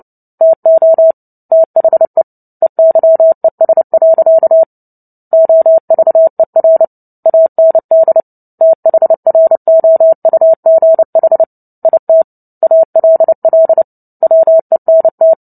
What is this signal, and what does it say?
Morse code, heard as a keyed tone, about 30 wpm, 650 Hz. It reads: TO THE EYES. OVER AND THROUGH IT ALL WENT
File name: Book Two / 03-Chapter 3 - 29 WPM